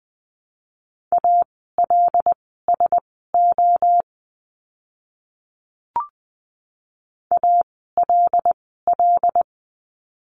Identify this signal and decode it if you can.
Morse code, heard as a keyed tone, about 20 wpm, spaced as for 15 wpm, 700 Hz. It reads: ALSO ALL
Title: 230 Lesson 37f - Letter x - Words 20wpm